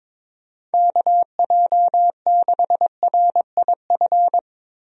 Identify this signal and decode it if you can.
Morse code, heard as a keyed tone, about 23 wpm, 700 Hz. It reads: KJ6RIF
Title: USA Call Signs - 6 Characters - 22wpm